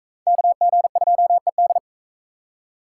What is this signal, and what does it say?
Morse code, heard as a keyed tone, about 40 wpm, 700 Hz. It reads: KG2ED